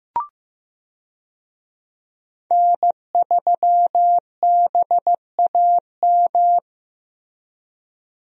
Morse code, heard as a keyed tone, about 15 wpm, 700 Hz. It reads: N3BAM
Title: USA Call Signs - 5 Characters - Repeated 2x - 15wpm